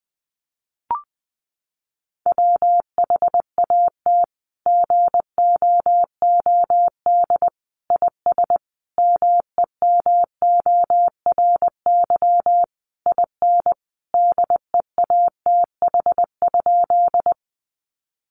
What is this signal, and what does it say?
Morse code, heard as a keyed tone, about 20 wpm, 700 Hz. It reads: WHAT GOOD IS MEMORY IN DEATH?
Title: Sentences from Top 1300 Words 20wpm